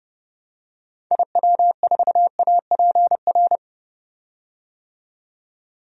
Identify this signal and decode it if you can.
Morse code, heard as a keyed tone, about 32 wpm, 700 Hz. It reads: IW4APR
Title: DXCC-100 Call Signs - 6 Characters - 30wpm